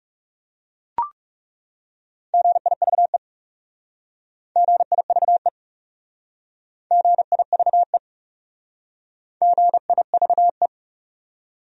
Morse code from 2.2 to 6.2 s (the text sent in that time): GIVE GIVE